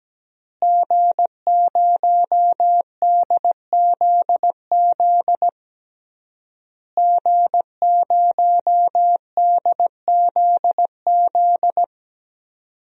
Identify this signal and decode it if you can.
Morse code, heard as a keyed tone, about 17 wpm, 700 Hz. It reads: G0DZZ G0DZZ
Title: DXCC-100 Call Signs - 5 Characters - Repeated 3x - 17wpm